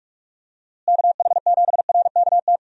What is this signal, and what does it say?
Morse code, heard as a keyed tone, about 46 wpm, 700 Hz. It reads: KH7RKT
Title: USA Call Signs - 6 Characters - Repeated 3x - 45wpm